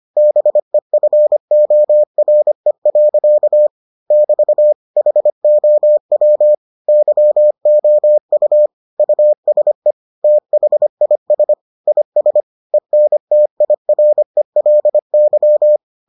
Morse code, heard as a keyed tone, about 25 wpm, 600 Hz. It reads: BEFORE. <BT> HOW YOU USE THIS IS ENTIRELY